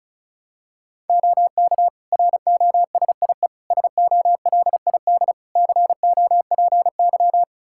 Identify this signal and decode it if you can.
Morse code, heard as a keyed tone, about 36 wpm, 700 Hz. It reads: OK ROSIE SOLID COPY
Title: Practice QSO - Exchange 3 - More Info - 35wpm